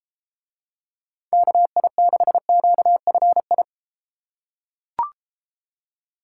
Morse code, heard as a keyed tone, about 32 wpm, 700 Hz. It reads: KI6QFI